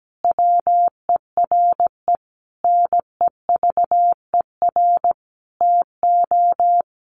Morse code, heard as a keyed tone, about 17 wpm, 700 Hz. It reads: WERE NEVER TO
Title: Sentences from Top 800 Words 17wpm